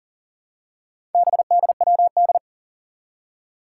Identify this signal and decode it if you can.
Morse code, heard as a keyed tone, about 38 wpm, 700 Hz. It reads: BDWD
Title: Four Challenging Letter-Number Combinations 40wpm